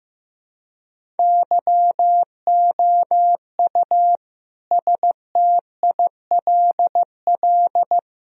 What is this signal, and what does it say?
Morse code, heard as a keyed tone, about 15 wpm, 700 Hz. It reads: YOU STILL
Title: Sentences from Top 1000 Words - Review Part 1 - 15wpm